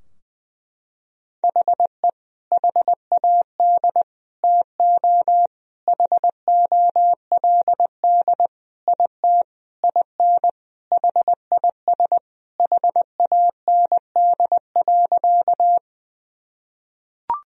Morse code, heard as a keyed tone, about 20 wpm, 700 Hz. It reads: HE HAD TO HOLD IT IN HIS HAND.
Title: Sentences from Top 1000 Words - Review Part 2 - 20wpm